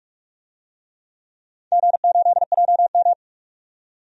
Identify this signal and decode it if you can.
Morse code, heard as a keyed tone, about 46 wpm, 700 Hz. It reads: G8JM
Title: DXCC-100 Call Signs - 4 Characters - 45wpm